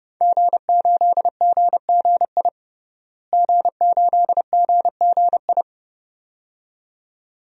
Morse code, heard as a keyed tone, about 32 wpm, 700 Hz. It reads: G8GGI G8GGI
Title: DXCC-100 Call Signs - 5 Characters - Repeated 2x - 30wpm